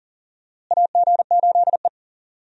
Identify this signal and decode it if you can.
Morse code, heard as a keyed tone, about 40 wpm, 700 Hz. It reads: AG8E
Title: USA Call Signs - 4 Characters - Repeated 2x - 40wpm